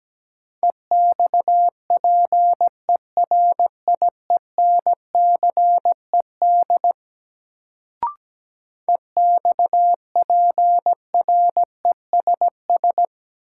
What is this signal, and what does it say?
Morse code, heard as a keyed tone, about 17 wpm, 700 Hz. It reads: EXPERIENCED EXPRESS
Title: Sentences from Top 1100 Words - New Word Drill Alt - 17wpm